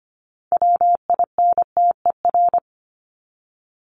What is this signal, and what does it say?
Morse code, heard as a keyed tone, about 25 wpm, 700 Hz. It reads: WINTER
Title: Top 1501-1750 Words 25wpm